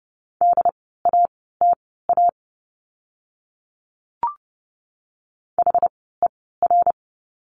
Morse code, heard as a keyed tone, about 30 wpm, spaced as for 17 wpm, 700 Hz. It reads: DATA HER
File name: Lesson 17f - Letter u - Words 30wpm